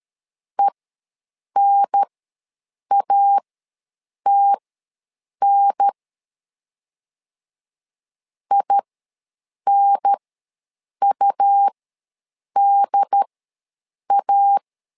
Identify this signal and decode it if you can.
Morse code, heard as a keyed tone, about 13 wpm, spaced as for 7 wpm, 800 Hz. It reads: ENATN INUDA
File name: LEZIONI / 2 UHDN / 2 UHDN ws 12wpm 1708